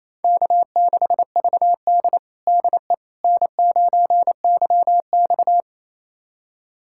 Morse code, heard as a keyed tone, about 28 wpm, 700 Hz. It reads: K6VD DE N9YX
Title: USA Callsign DE Callsign - 4 Characters - 28wpm